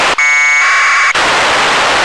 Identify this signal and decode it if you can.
Packet radio (AX.25, as in APRS).
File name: lu4aao / globo19 1 LU2DPW